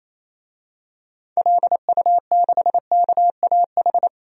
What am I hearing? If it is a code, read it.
Morse code, heard as a keyed tone, about 28 wpm, 700 Hz. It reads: LU6KAH
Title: DXCC-100 Call Signs - 6 Characters - Repeated 2x - 28wpm